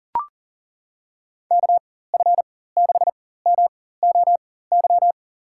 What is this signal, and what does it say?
Morse code, heard as a keyed tone, about 40 wpm, spaced as for 19 wpm, 700 Hz. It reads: KF6MOY
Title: USA Call Signs - 6 Characters - 4x Character Spacing - 40wpm